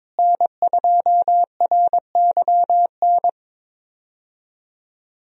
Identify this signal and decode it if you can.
Morse code, heard as a keyed tone, about 22 wpm, 700 Hz. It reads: N2RYN
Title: USA Call Signs - 5 Characters - Repeated 3x - 22wpm